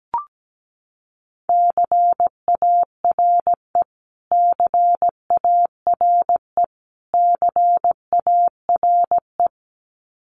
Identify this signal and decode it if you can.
Morse code, heard as a keyed tone, about 17 wpm, 700 Hz. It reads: CARE CARE CARE
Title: Top 500 Words Repeated 17wpm